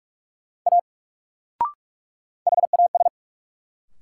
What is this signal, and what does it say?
Morse code, heard as a keyed tone, about 47 wpm, 700 Hz. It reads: A HAS
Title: Sentences from Top 100 Words - New Word Drill - 45wpm